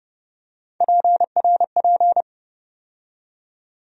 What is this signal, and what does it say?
Morse code, heard as a keyed tone, about 29 wpm, 700 Hz. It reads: PRP
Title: USA Call Sign Postfixes - 3 Letters 30wpm